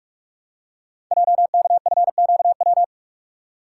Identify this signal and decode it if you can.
Morse code, heard as a keyed tone, about 44 wpm, 700 Hz. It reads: JKFQW
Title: Code Groups - 5 Letters - 45wpm